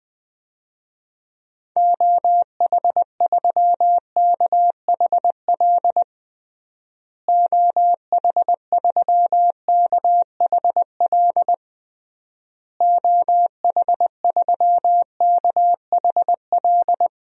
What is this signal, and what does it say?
Morse code, heard as a keyed tone, about 20 wpm, 700 Hz. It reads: OH3KHL OH3KHL OH3KHL
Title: DXCC-100 Call Signs - 6 Characters - Repeated 3x - 20wpm